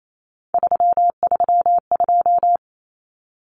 Morse code, heard as a keyed tone, about 28 wpm, 700 Hz. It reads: 332
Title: Common Cut Numbers - 3-Digit 28wpm